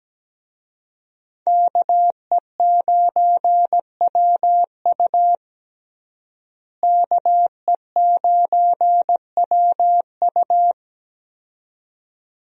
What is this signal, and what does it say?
Morse code, heard as a keyed tone, about 17 wpm, 700 Hz. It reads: KE9WU KE9WU
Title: USA Call Signs - 5 Characters - Repeated 2x - 17wpm